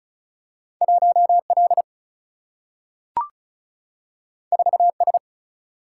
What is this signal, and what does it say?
Morse code, heard as a keyed tone, about 36 wpm, 700 Hz. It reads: 1 L 4 S
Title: Single-Letter Single-Number 35wpm